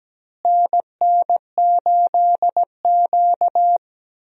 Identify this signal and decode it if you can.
Morse code, heard as a keyed tone, about 17 wpm, 700 Hz. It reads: NN8Q
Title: USA Call Signs - 4 Characters - Repeated 2x - 17wpm